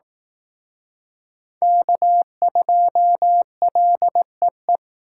Morse code, heard as a keyed tone, about 19 wpm, 700 Hz. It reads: K2LEE
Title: Speed Racing - USA 1x3 Call Signs 15wpm